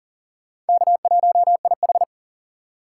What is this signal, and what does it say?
Morse code, heard as a keyed tone, about 50 wpm, 700 Hz. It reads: K1IH
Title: Speed Racing - USA 1x2 Call Signs 40wpm